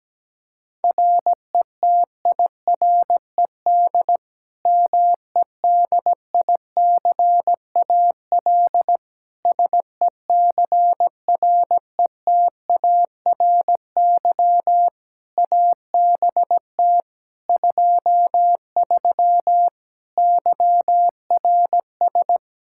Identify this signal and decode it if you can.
Morse code, heard as a keyed tone, about 17 wpm, 700 Hz. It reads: RETIRED MEDICAL SECRETARY ABT 23 YRS
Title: K1USN SST QSO Practice 17wpm